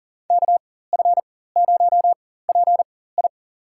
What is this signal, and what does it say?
Morse code, heard as a keyed tone, about 40 wpm, spaced as for 19 wpm, 700 Hz. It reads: KF0PI
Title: USA Call Signs - 5 Characters - 4x Character Spacing - 40wpm